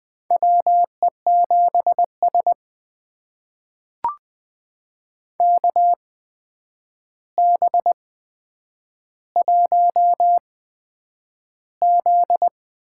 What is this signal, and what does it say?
Morse code, heard as a keyed tone, about 20 wpm, spaced as for 9 wpm, 700 Hz. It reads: WE7S KB1Z
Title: USA Call Signs - 4 Characters - 8x Character Spacing - 20wpm